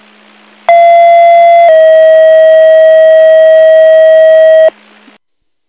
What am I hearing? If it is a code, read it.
Two-tone page, heard as a one-second tone, then a three-second tone, about 680 then 640 Hz. It is heard from 0.7 to 4.7 s.